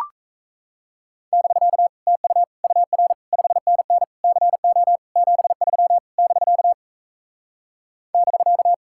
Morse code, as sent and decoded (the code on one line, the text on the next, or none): -...-.- / - ..- / ..- .-. / ..... -. -. / -.-. --- / --... ...-- / -...-.- / -...-.-
<BK> TU UR 5NN CO 73 <BK> <BK>